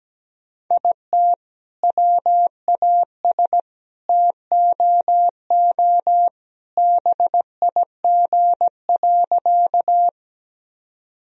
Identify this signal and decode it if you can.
Morse code, heard as a keyed tone, about 17 wpm, 700 Hz. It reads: IT WAS TOO BIG.
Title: Sentences from Top 1000 Words - Review Part 1 - 17wpm